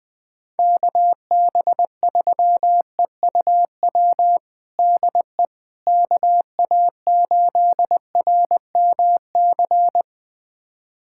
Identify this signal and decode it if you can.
Morse code, heard as a keyed tone, about 20 wpm, 700 Hz. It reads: KB3EUW DE KA8RMC
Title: USA Callsign DE Callsign - 6 Characters - 20wpm